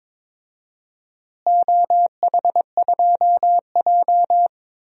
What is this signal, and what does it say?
Morse code, heard as a keyed tone, about 22 wpm, 700 Hz. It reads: OH2J
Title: DXCC-100 Call Signs - 4 Characters - Repeated 2x - 22wpm